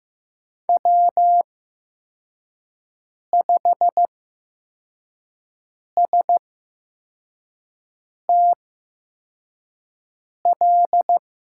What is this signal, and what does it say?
Morse code, heard as a keyed tone, about 15 wpm, spaced as for 4 wpm, 700 Hz. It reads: W5STL
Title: USA Call Signs - 5 Characters - 8x Character Spacing - 15wpm